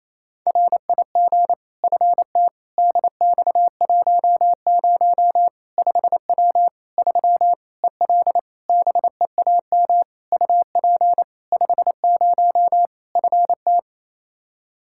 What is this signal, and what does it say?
Morse code, heard as a keyed tone, about 27 wpm, 700 Hz. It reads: RIG FT DX10 5W 3 EL BEAM UP 50 FT
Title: Practice QSO - Exchange 2 - Info - 28wpm